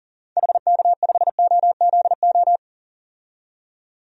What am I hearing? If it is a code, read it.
Morse code, heard as a keyed tone, about 40 wpm, 700 Hz. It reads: HK5OZO
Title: DXCC-100 Call Signs - 6 Characters - 40wpm